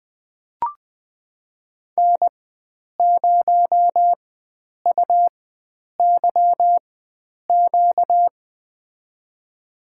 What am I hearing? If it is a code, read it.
Morse code, heard as a keyed tone, about 20 wpm, spaced as for 9 wpm, 700 Hz. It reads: N0UYQ
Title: USA Call Signs - 4x Character Spacing - 20wpm